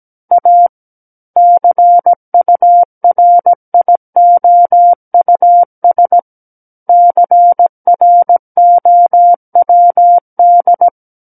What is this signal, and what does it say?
Morse code, heard as a keyed tone, about 17 wpm, 700 Hz. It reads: A CURIOUS CROWD
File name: Book One / 08-Chapter 8 - 17 WPM